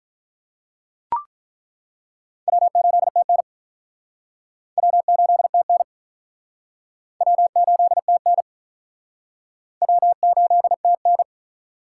Morse code, heard as a keyed tone, about 47 wpm, 700 Hz. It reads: W8TN W8TN W8TN W8TN